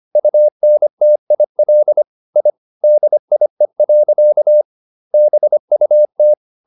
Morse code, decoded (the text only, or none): UNTIL I DIE. BUT